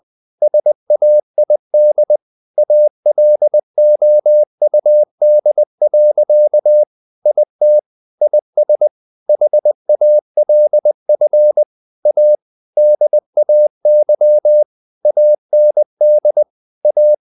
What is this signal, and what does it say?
Morse code, heard as a keyed tone, about 20 wpm, 600 Hz. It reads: SAID ALOUD. IT IS HALF A DAY AND A
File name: the old man and the sea part 2 20wpm